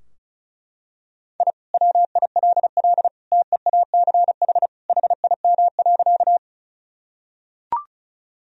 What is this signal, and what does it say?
Morse code, heard as a keyed tone, about 34 wpm, 700 Hz. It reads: I WILL TEACH HIM.
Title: Sentences from Top 1000 Words - Review Part 2 - 35wpm